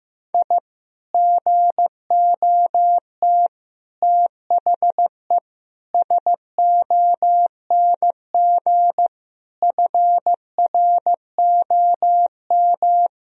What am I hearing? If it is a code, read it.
Morse code, heard as a keyed tone, about 15 wpm, 700 Hz. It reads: I GOT THE SONG FROM
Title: Sentences from Top 1000 Words - Review Part 2 - 15wpm